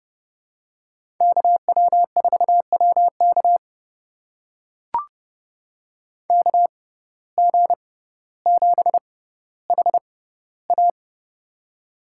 Morse code, heard as a keyed tone, about 30 wpm, spaced as for 15 wpm, 700 Hz. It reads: KW4WK KG7HA
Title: USA Call Signs - 5 Characters - 6x Character Spacing - 30wpm